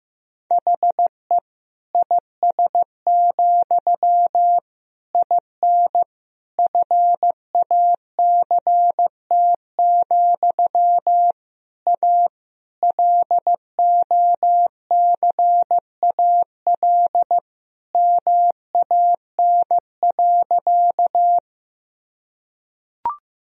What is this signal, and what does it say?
Morse code, heard as a keyed tone, about 15 wpm, 700 Hz. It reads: HE IS, IN FACT, A LOCAL MAN.
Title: Sentences from Top 1000 Words - Review Part 3 - 15wpm